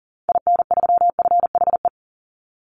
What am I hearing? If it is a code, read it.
Morse code, heard as a keyed tone, about 40 wpm, 700 Hz. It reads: IN3FHE
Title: Call Signs Repeated 2x - 40wpm